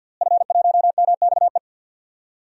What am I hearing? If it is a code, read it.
Morse code, heard as a keyed tone, about 51 wpm, 700 Hz. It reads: F1MXE
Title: DXCC-100 Call Signs - 5 Characters - Repeated 3x - 50wpm